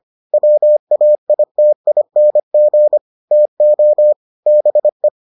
Morse code, heard as a keyed tone, about 25 wpm, 600 Hz. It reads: WAITING TO BE